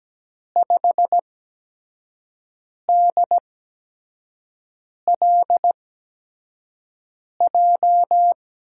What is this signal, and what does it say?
Morse code, heard as a keyed tone, about 17 wpm, spaced as for 5 wpm, 700 Hz. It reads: 5DLJ
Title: USA Call Signs - 6 Characters - 8x Character Spacing - 17wpm